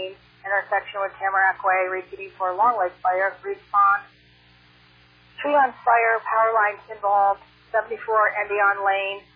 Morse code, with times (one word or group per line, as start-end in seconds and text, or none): none